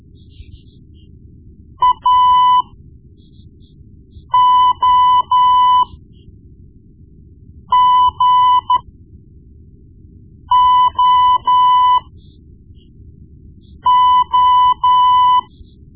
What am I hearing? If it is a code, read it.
Morse code, heard as a keyed tone, about 13 wpm, spaced as for 4 wpm, 1000 Hz. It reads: AOGOO
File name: Morse Lessons / Morse Lessons / Tape-1B